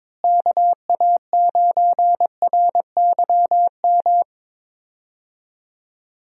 Morse code, heard as a keyed tone, about 22 wpm, 700 Hz. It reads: KA9RYM
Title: USA Call Signs - 6 Characters - Repeated 3x - 22wpm